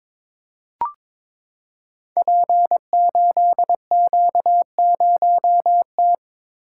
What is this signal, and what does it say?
Morse code, heard as a keyed tone, about 23 wpm, 700 Hz. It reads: P8Q0T
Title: Code Groups - 5 Letter:Numbers - 22wpm